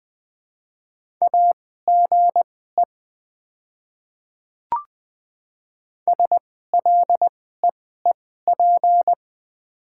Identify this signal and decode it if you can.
Morse code, heard as a keyed tone, about 20 wpm, spaced as for 14 wpm, 690 Hz. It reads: AGE SLEEP
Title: 2x Character Spacing - Top 500 Words - 20wpm